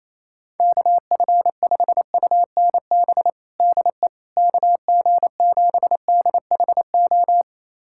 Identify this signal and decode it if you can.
Morse code, heard as a keyed tone, about 28 wpm, 700 Hz. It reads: KF5UNB DE KG7DHO
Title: USA Callsign DE Callsign - 6 Characters - 28wpm